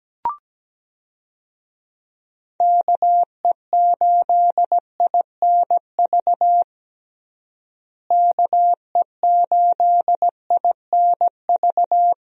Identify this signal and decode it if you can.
Morse code, heard as a keyed tone, about 17 wpm, 700 Hz. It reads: KE8INV KE8INV
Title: USA Call Signs - 6 Characters - Repeated 3x - 17wpm